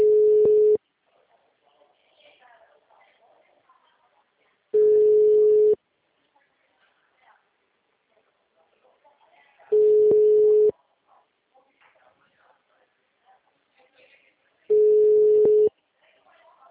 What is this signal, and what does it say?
Ringback tone (Europe, 425 Hz), with speech.
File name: out- -2013- - - .13342